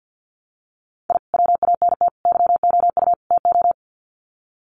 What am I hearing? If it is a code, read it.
Morse code, heard as a keyed tone, about 50 wpm, 700 Hz. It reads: I WANT YOU TO
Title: Sets of 4 Words 50wpm